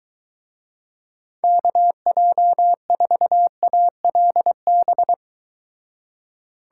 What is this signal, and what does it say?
Morse code, heard as a keyed tone, about 23 wpm, 700 Hz. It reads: KJ4ALB